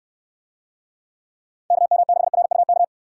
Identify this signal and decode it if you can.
Morse code, heard as a keyed tone, about 68 wpm, 700 Hz. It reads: KM6RUK